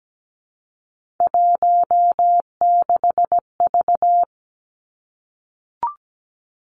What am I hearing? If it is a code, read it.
Morse code, heard as a keyed tone, about 17 wpm, 700 Hz. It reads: 16V
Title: Three Challenging Letter-Number Combinations 17wpm